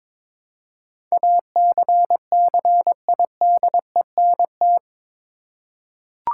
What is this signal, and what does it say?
Morse code, heard as a keyed tone, about 21 wpm, 700 Hz. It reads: ACCIDENT
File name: practice / 22wpm / Eight Letter Words - 22wpm